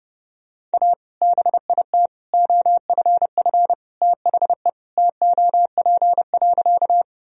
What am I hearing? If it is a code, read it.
Morse code, heard as a keyed tone, about 30 wpm, 700 Hz. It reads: A BIT OFF THE TOP.
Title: Sentences from Top 1000 Words - Review Part 2 - 30wpm